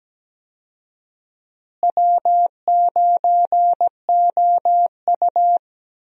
Morse code, heard as a keyed tone, about 17 wpm, 700 Hz. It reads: W9OU
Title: USA Call Signs - 4 Characters - Repeated 2x - 17wpm